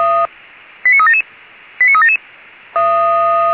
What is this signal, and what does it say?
Five-tone selective call (ZVEI), heard as quick runs of five tones, tones of about 70 ms. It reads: 89377 89377
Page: SDR/ZVEI